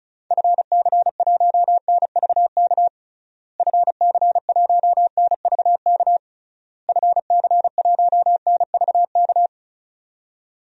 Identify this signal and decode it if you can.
Morse code, heard as a keyed tone, about 34 wpm, 700 Hz. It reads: FC1NVK FC1NVK FC1NVK
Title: DXCC-100 Call Signs - 6 Characters - Repeated 3x - 35wpm